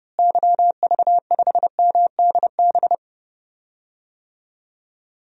Morse code, heard as a keyed tone, about 32 wpm, 700 Hz. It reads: YV5MDB
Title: DXCC-100 Call Signs - 6 Characters - Repeated 2x - 30wpm